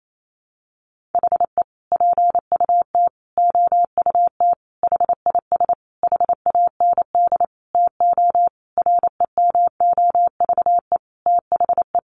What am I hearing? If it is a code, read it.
Morse code, heard as a keyed tone, about 28 wpm, 700 Hz. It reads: HE PUT OUT HIS HAND TO REMOVE THE
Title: Sentences from Top 800 Words 28wpm